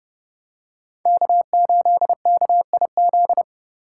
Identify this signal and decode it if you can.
Morse code, heard as a keyed tone, about 30 wpm, 700 Hz. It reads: K8KIZ